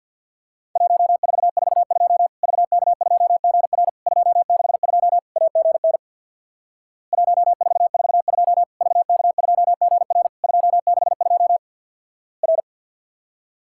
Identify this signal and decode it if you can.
Morse code, heard as a keyed tone, about 50 wpm, 690 Hz. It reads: 1442 VK2GR 262 AGN 1442 VK2GR 262 R
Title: Worked All Europe DX Contest - QTC Exchange with Repeats - 50wpm